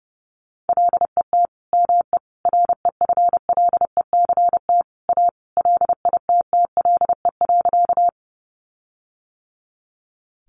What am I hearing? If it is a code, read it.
Morse code, heard as a keyed tone, about 30 wpm, 700 Hz. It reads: LET ME REFLECT A LITTLE.
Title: Sentences from Top 1500 Words 30wpm